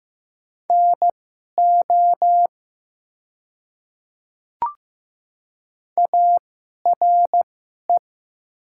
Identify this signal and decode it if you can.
Morse code, heard as a keyed tone, about 15 wpm, spaced as for 11 wpm, 700 Hz. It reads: NO ARE
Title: 2x Character Spacing - Top 100 Words - 15wpm